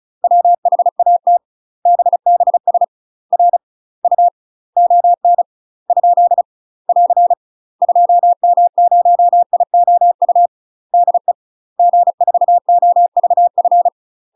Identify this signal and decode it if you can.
Morse code, heard as a keyed tone, about 36 wpm, 700 Hz. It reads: WHAT BBS R U ON ? <AR> 2M0IOU DE G4OVF
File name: UK MASTER QSOs TRAINING / 35WPM / UK MASTER QSOs 0033